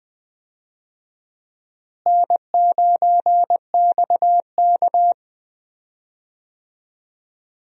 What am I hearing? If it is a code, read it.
Morse code, heard as a keyed tone, about 20 wpm, 700 Hz. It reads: N9XK
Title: USA Call Signs - 4 Characters - Repeated 2x - 20wpm